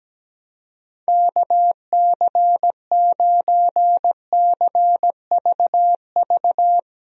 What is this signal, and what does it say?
Morse code, heard as a keyed tone, about 17 wpm, 700 Hz. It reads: KC9CVV
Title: USA Call Signs - 6 Characters - Repeated 3x - 17wpm